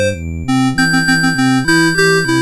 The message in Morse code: none